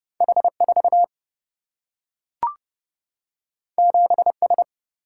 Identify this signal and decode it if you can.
Morse code, heard as a keyed tone, about 30 wpm, 700 Hz. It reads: H 4 7 S
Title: Single-Letter Single-Number 30wpm